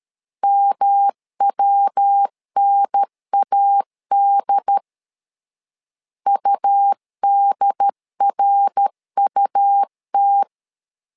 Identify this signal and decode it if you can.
Morse code, heard as a keyed tone, about 13 wpm, 800 Hz. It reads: MWNAD UDRUT
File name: LEZIONI / 3 MWRO / 3 MWR 12wpm 1711